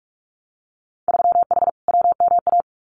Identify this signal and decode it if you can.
Morse code, heard as a keyed tone, about 46 wpm, 700 Hz. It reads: 3H WMA
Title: Winter Field Day Exchanges Encore 45wpm